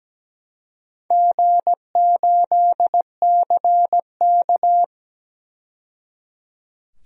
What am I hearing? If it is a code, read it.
Morse code, heard as a keyed tone, about 17 wpm, 700 Hz. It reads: G8CK
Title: DXCC-100 Call Signs - 4 Characters - 17wpm